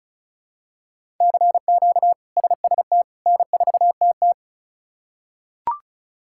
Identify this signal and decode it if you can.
Morse code, heard as a keyed tone, about 34 wpm, 700 Hz. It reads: CQ SST N4TT
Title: K1USN SST QSO Practice - 35wpm